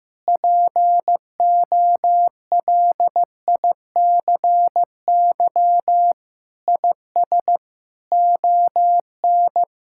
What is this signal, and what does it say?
Morse code, heard as a keyed tone, about 15 wpm, 700 Hz. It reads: POLICY IS ON
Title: Sentences from Top 1000 Words - Review Part 5 - 15wpm